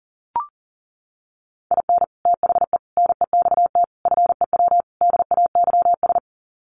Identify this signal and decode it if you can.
Morse code, heard as a keyed tone, about 40 wpm, 700 Hz. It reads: IN THE NEXT FEW DAYS
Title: Sets of 5 Words Encore2 40wpm